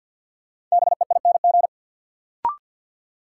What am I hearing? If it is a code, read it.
Morse code, heard as a keyed tone, about 53 wpm, 700 Hz. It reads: BEING
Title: Sentences from Top 600 Words - New Word Drill - 50wpm